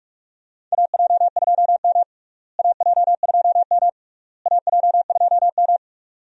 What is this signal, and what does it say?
Morse code, heard as a keyed tone, about 42 wpm, 700 Hz. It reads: AJ2M AJ2M AJ2M
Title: USA Call Signs - 4 Characters - Repeated 3x - 45wpm